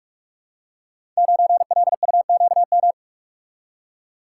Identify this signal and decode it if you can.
Morse code, heard as a keyed tone, about 45 wpm, 700 Hz. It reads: 9LUQM
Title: Code Groups - 5 Letter:Numbers - 45wpm